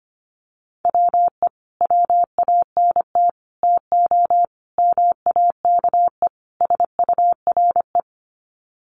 Morse code, heard as a keyed tone, about 25 wpm, 700 Hz. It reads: WE WANT TO MAKE SURE
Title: Sets of 5 Words Encore2 25wpm